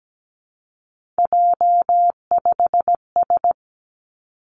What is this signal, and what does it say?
Morse code, heard as a keyed tone, about 17 wpm, 700 Hz. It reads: J5S
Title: Three Challenging Letter-Number Combinations 17wpm